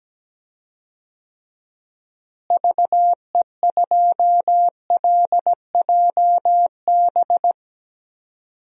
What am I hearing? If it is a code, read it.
Morse code, heard as a keyed tone, about 17 wpm, 700 Hz. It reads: VE2LJB
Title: DXCC-100 Call Signs - 6 Characters - Repeated 2x - 17wpm